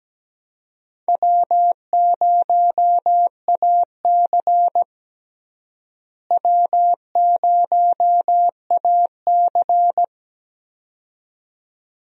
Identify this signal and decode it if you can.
Morse code, heard as a keyed tone, about 17 wpm, 700 Hz. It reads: W0AC W0AC
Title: USA Call Signs - 4 Characters - Repeated 2x - 17wpm